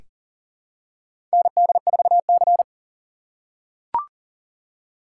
Morse code, heard as a keyed tone, about 40 wpm, 700 Hz. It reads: ND4C
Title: USA Call Signs - 4 Characters - 2x Character Spacing - 40wpm